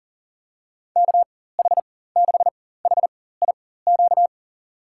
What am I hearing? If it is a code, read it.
Morse code, heard as a keyed tone, about 40 wpm, spaced as for 19 wpm, 700 Hz. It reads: KH6HIQ
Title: USA Call Signs - 4x Character Spacing - 40wpm